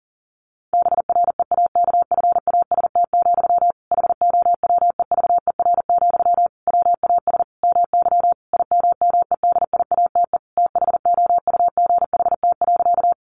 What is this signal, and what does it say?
Morse code, heard as a keyed tone, about 40 wpm, 700 Hz. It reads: BREAKFAST, HOWEVER, WAS MY IMMEDIATE THOUGHT.
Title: Sentences from Top 1500 Words 40wpm